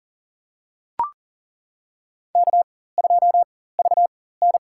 Morse code, heard as a keyed tone, about 37 wpm, spaced as for 18 wpm, 700 Hz. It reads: K2VN
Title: USA Call Signs - 4 Characters - 4x Character Spacing - 40wpm